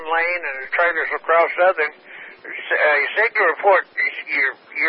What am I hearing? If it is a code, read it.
Emergency Alert System (SAME) tones (header bursts), heard as harsh buzzing bursts, with speech. It is heard from 0.2 to 4.5 s.